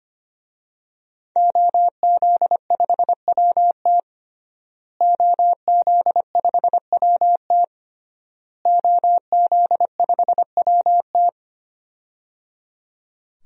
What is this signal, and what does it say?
Morse code, heard as a keyed tone, about 25 wpm, 700 Hz. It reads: OZ5WT OZ5WT OZ5WT
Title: DXCC-100 Call Signs - 5 Characters - Repeated 3x - 25wpm